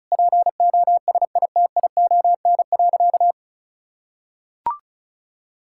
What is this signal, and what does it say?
Morse code, heard as a keyed tone, about 36 wpm, 700 Hz. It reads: POSITION.
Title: Sentences from Top 1000 Words - Review Part 5 - 35wpm